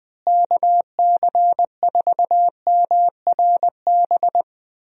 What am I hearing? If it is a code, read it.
Morse code, heard as a keyed tone, about 20 wpm, 700 Hz. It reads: KC4MRB
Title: USA Call Signs - 6 Characters - Repeated 2x - 20wpm